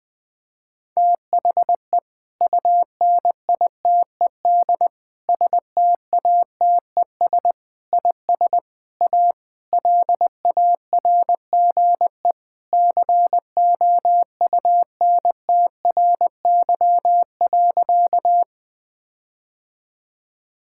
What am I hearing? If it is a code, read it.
Morse code, heard as a keyed tone, about 20 wpm, 700 Hz. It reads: THE UNITED STATES IS A LARGE COUNTRY.
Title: Sentences with 4-Word Phrases - Spoken Beginning - 20wpm